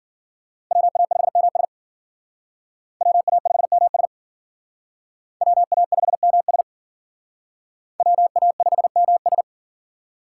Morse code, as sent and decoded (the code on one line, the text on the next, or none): .-- .- ..... -- ... / .-- .- ..... -- ... / .-- .- ..... -- ... / .-- .- ..... -- ...
WA5MS WA5MS WA5MS WA5MS